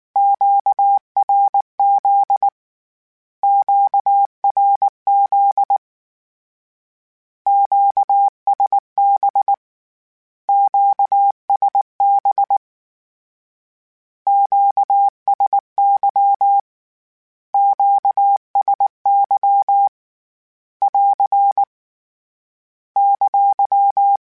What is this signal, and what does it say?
Morse code, heard as a keyed tone, about 19 wpm, 800 Hz. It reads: QRZ QRZ QSB QSB QSY QSY <AR> !
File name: Q Calls Twice @18